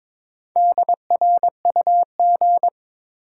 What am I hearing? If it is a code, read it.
Morse code, heard as a keyed tone, about 22 wpm, 700 Hz. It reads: DRUG
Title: Sets of 5 Words - New Word Drill - 22wpm